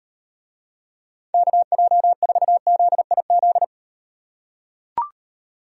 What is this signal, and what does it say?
Morse code, heard as a keyed tone, about 38 wpm, 700 Hz. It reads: KJ4ZIZ